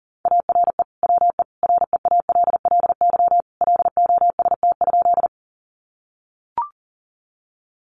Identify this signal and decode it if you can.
Morse code, heard as a keyed tone, about 40 wpm, 700 Hz. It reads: ARE WE REALLY LOST?
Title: Sentences from Top 1400 Words 40wpm